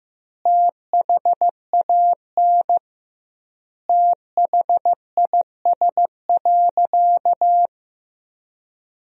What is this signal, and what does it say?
Morse code, heard as a keyed tone, about 15 wpm, 700 Hz. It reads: THAN THIS.
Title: 2x Word-Spacing - Sentences from Top 100 Words - 15wpm